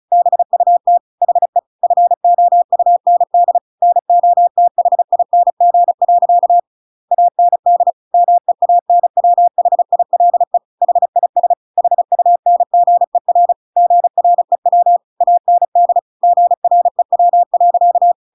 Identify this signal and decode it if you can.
Morse code, heard as a keyed tone, about 36 wpm, 700 Hz. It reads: BUT HE FOUND NOTHING. AND MEANWHILE HIS HUNGER GREW AND GREW.